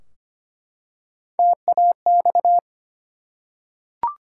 Morse code, heard as a keyed tone, about 25 wpm, 700 Hz. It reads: TAX
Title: Three Letter Words - 25wpm